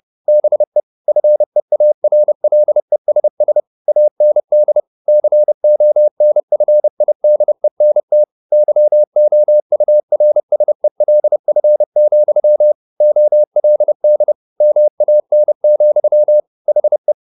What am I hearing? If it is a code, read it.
Morse code, heard as a keyed tone, about 30 wpm, 600 Hz. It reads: BE FEARLESS AND CONFIDENT YOURSELF, OLD MAN, HE